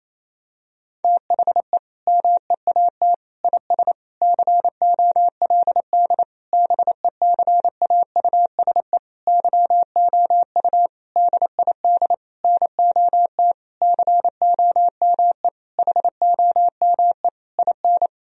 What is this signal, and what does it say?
Morse code, heard as a keyed tone, about 28 wpm, 700 Hz. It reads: THE MEAT IS COLD BECAUSE YOU DID NOT COME HOME IN
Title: Sentences from Top 1000 Words - Review Part 3 - 28wpm